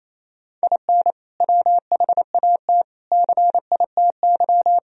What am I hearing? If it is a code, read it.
Morse code, heard as a keyed tone, about 28 wpm, 700 Hz. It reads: IN WHAT CITY
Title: Sentences from Top 1000 Words - Review Part 1 - 28wpm